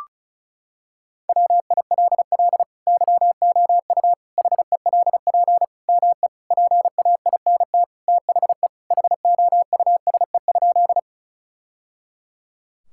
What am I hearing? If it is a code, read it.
Morse code, heard as a keyed tone, about 34 wpm, 690 Hz. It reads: WILL YOU HELP ME PAINT THE HOUSE?
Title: Sentences from Top 1000 Words - Review Part 6 - 35wpm